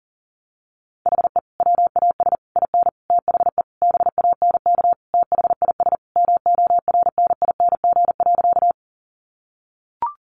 Morse code, heard as a keyed tone, about 40 wpm, 700 Hz. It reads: HE WAS IN THE BANK THIS MORNING.
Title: Sentences from Top 900 Words 40wpm